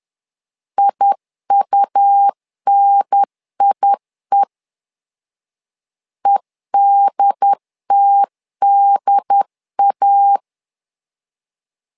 Morse code, heard as a keyed tone, about 10 wpm, 800 Hz. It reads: IUNIE EDTDA
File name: LEZIONI / 2 UHDN / 2 UHDN 10wpm 1904